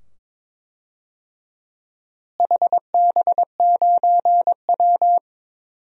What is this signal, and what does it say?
Morse code, heard as a keyed tone, about 22 wpm, 700 Hz. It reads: HB9W
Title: DXCC-100 Call Signs - 4 Characters - Repeated 2x - 22wpm